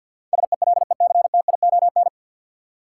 Morse code, heard as a keyed tone, about 51 wpm, 700 Hz. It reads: SELECTION
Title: Sentences from Top 1400 Words - New Word Drill Alt - 50wpm